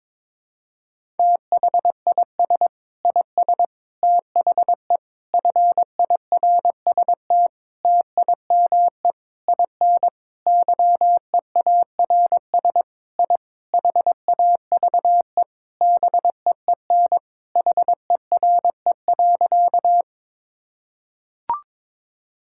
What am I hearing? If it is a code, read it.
Morse code, heard as a keyed tone, about 22 wpm, 700 Hz. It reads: THIS IS THE FIRST TIME IN YEARS I HAVE BEEN HERE.
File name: practice / 22wpm / Sentences with 4-Word Phrases - 22wpm